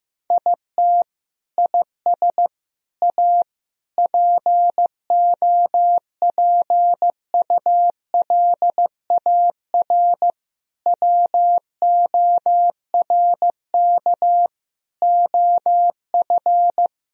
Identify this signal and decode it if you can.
Morse code, heard as a keyed tone, about 15 wpm, 700 Hz. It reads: IT IS A POPULAR WORK OF
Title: Sentences from Top 1000 Words - Review Part 3 - 15wpm